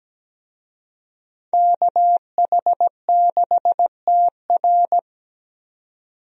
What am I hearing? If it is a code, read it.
Morse code, heard as a keyed tone, about 17 wpm, 700 Hz. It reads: KH6TR
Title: DXCC-100 Call Signs - 5 Characters - 17wpm